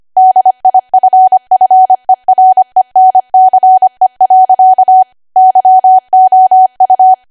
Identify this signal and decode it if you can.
Morse code, heard as a keyed tone, about 25 wpm, 750 Hz. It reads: DIFFERENCE. YOU
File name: Archive / 25 WPM / 25WPM